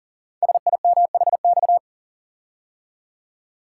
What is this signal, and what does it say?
Morse code, heard as a keyed tone, about 40 wpm, 700 Hz. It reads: SIMHX
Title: Code Groups - 5 Letters - 40wpm